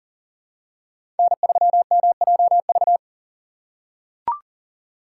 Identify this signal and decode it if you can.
Morse code, heard as a keyed tone, about 40 wpm, 690 Hz. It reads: N3MJV
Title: USA Call Signs - 2x Character Spacing - 40wpm